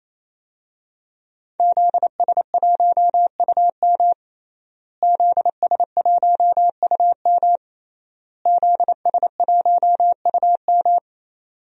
Morse code, heard as a keyed tone, about 28 wpm, 700 Hz. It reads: ZS1UM ZS1UM ZS1UM
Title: DXCC-100 Call Signs - 5 Characters - Repeated 3x - 28wpm